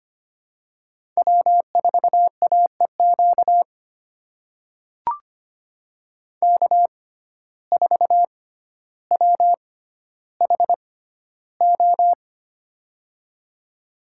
Morse code, heard as a keyed tone, about 25 wpm, spaced as for 13 wpm, 700 Hz. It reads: W4AEQ K4WHO
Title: USA Call Signs - 5 Characters - 6x Character Spacing - 25wpm